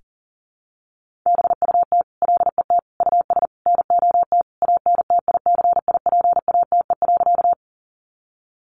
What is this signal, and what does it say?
Morse code, heard as a keyed tone, about 40 wpm, 700 Hz. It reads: BUT LET US NOT ANTICIPATE.
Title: Sentences from Top 1500 Words 40wpm